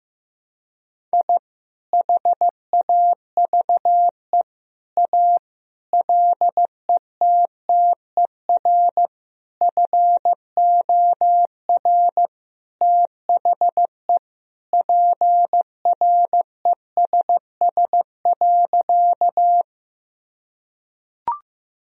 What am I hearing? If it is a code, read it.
Morse code, heard as a keyed tone, about 16 wpm, 700 Hz. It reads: I HAVE A LETTER FOR THE PRESS.
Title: Sentences from Top 1000 Words - Review Part 1 - 15wpm